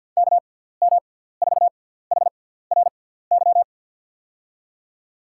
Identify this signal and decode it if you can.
Morse code, heard as a keyed tone, about 51 wpm, spaced as for 17 wpm, 700 Hz. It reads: KM4HRY
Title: USA Call Signs - 6 Characters - 6x Character Spacing - 50wpm